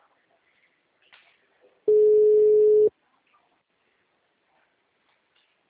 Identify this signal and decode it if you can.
Ringback tone (Europe, 425 Hz).
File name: out- -2011